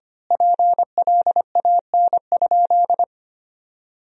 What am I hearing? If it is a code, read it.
Morse code, heard as a keyed tone, about 25 wpm, 700 Hz. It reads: PLAN?
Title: Sentences from Top 1000 Words - Review Part 4 - 25wpm